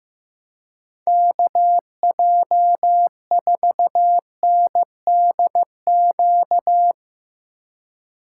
Morse code, heard as a keyed tone, about 16 wpm, 700 Hz. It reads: KJ4NDQ
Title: Speed Racing - USA Call Signs 15wpm